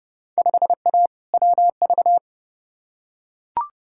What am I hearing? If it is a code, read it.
Morse code, heard as a keyed tone, about 30 wpm, 700 Hz. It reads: 5A WV
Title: Field Day Exchanges - 30wpm